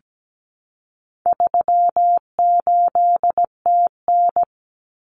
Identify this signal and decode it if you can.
Morse code, heard as a keyed tone, about 17 wpm, 700 Hz. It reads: 38TN
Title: Common Cut Numbers - 4-Digit 17wpm